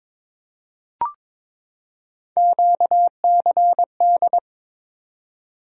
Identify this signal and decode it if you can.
Morse code, heard as a keyed tone, about 22 wpm, 700 Hz. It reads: QCD
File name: practice / 22wpm / Three Letter-Number Combinations - 22wpm